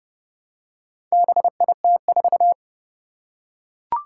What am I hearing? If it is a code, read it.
Morse code, heard as a keyed tone, about 31 wpm, 700 Hz. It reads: BIT4
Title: Four Challenging Letter-Number Combinations 30wpm